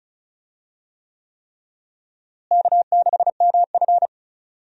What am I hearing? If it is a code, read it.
Morse code, heard as a keyed tone, about 34 wpm, 700 Hz. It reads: K6MF
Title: USA Call Signs - 4 Characters - Repeated 3x - 35wpm